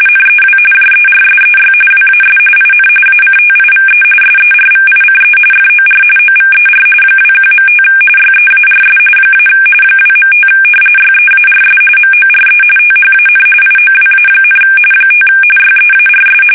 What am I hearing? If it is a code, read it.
RTTY, about 45.45 baud. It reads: NEW JERSEY WAS IN ACTIVE SERVICE FOR OVER 21.5 YEARS - 2.5 YEARS MORE THAN IOWA, 5 YEARS
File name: fsk850 / battleshipnj